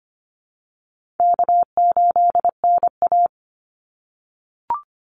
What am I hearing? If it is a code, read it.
Morse code, heard as a keyed tone, about 25 wpm, 700 Hz. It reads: K8NA
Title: Easy Call Signs - 25wpm